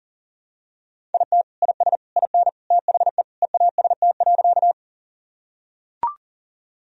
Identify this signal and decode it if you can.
Morse code, heard as a keyed tone, about 40 wpm, 700 Hz. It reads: IT IS IN THE EAST.
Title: Sentences from Top 1000 Words - Review Part 3 - 40wpm